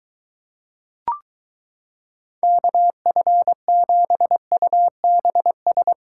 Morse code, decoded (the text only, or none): KF7UBS